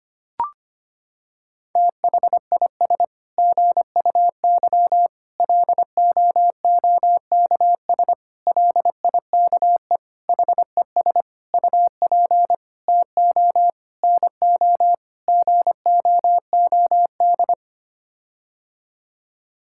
Morse code, heard as a keyed tone, about 25 wpm, 700 Hz. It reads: THIS GUY LOOKS LIKE HES UP TO NO GOOD
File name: Sets of 9 Words - 25wpm